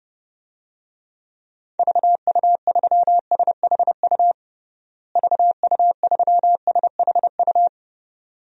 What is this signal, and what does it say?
Morse code, heard as a keyed tone, about 30 wpm, 700 Hz. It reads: VU3SHU VU3SHU
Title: DXCC-100 Call Signs - 6 Characters - Repeated 2x - 30wpm